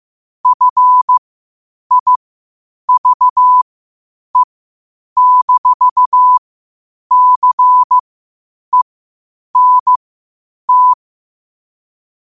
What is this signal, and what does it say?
Morse code, heard as a keyed tone, about 15 wpm, spaced as for 8 wpm, 1000 Hz. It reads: FIVE-CENT
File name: qotdfiles / 8wpm